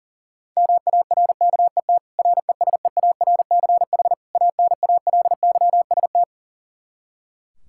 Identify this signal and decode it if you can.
Morse code, heard as a keyed tone, about 40 wpm, 700 Hz. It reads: MARKET RESEARCH ANALYST
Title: Practice QSO - Exchange 3 - More Info - 40wpm